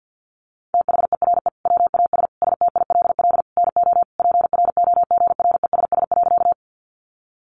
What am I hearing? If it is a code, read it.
Morse code, heard as a keyed tone, about 48 wpm, 700 Hz. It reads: THERE WAS STILL NO PROGRESS.
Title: Sentences from Top 1400 Words 50wpm